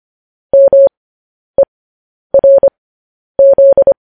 Morse code, heard as a keyed tone, about 25 wpm, spaced as for 10 wpm, 550 Hz. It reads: MERZ
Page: news-headlines-in-morse-code-at-10-wpm